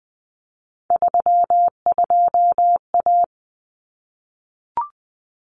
Morse code, heard as a keyed tone, about 20 wpm, 700 Hz. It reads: 32A
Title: Common Cut Numbers - 3-Digit 20wpm